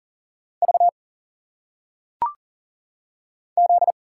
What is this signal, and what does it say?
Morse code, heard as a keyed tone, about 40 wpm, 700 Hz. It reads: V Z
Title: Single Letters - Phonetic Pronunciation 40wpm